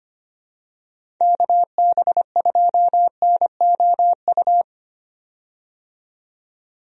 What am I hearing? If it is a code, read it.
Morse code, heard as a keyed tone, about 25 wpm, 700 Hz. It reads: KB2NOU
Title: USA Call Signs - 6 Characters - 25wpm